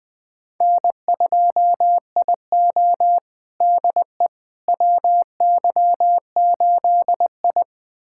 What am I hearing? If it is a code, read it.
Morse code, heard as a keyed tone, about 20 wpm, 700 Hz. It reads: N2IO DE WY8I
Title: USA Callsign DE Callsign - 4 Characters - 20wpm